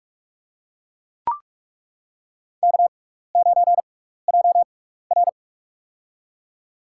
Morse code, heard as a keyed tone, about 42 wpm, spaced as for 15 wpm, 700 Hz. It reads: K9JR
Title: USA Call Signs - 4 Characters - 6x Character Spacing - 45wpm